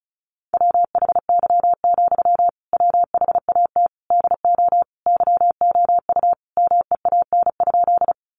Morse code, heard as a keyed tone, about 34 wpm, 700 Hz. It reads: WHY, WHAT DO YOU MEAN?
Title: Sentences from Top 200 Words 35wpm